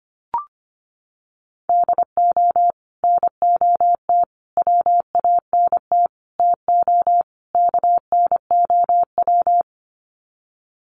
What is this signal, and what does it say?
Morse code, heard as a keyed tone, about 25 wpm, 700 Hz. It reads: DO NOT WANT TO KNOW
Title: Sets of 5 Words Encore 25wpm